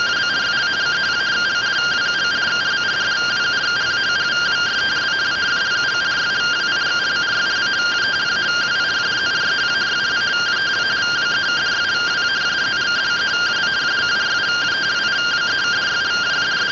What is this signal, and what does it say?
Two-tone teleprinter signal, about 170 Hz shift.
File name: findings / 81 axial iodine 262.gen-44 rtty rx